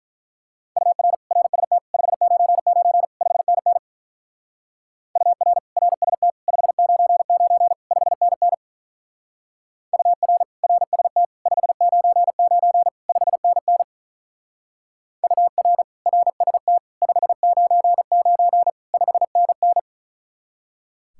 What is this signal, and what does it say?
Morse code, heard as a keyed tone, about 43 wpm, 700 Hz. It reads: UR RST 599 5NN UR RST 599 5NN UR RST 599 5NN UR RST 599 5NN
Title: Speed-Racing Common QSO Phrases Encore - 35wpm